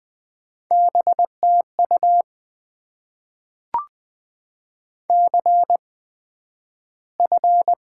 Morse code, heard as a keyed tone, about 20 wpm, spaced as for 11 wpm, 700 Hz. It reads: BTU CF
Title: Top 90 QSO Elements - 8x Character Spacing - 20wpm